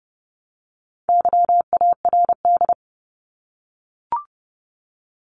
Morse code, heard as a keyed tone, about 30 wpm, 700 Hz. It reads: YARD
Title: Top 1751-2000 Words 30wpm